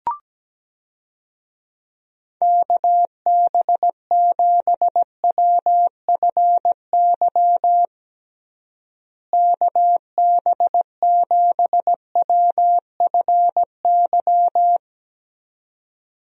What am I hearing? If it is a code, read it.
Morse code, heard as a keyed tone, about 17 wpm, 700 Hz. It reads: KB7WFY KB7WFY
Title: USA Call Signs - 6 Characters - Repeated 2x - 17wpm